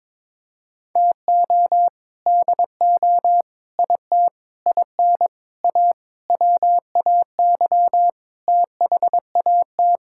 Morse code, heard as a keyed tone, about 21 wpm, 700 Hz. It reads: TO DO IT IN A WAY THAT
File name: practice / 22wpm / Sets of 7 Words - 22wpm